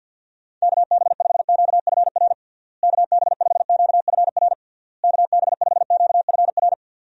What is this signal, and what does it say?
Morse code, heard as a keyed tone, about 52 wpm, 700 Hz. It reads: KB5QFR KB5QFR KB5QFR
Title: USA Call Signs - 6 Characters - Repeated 3x - 50wpm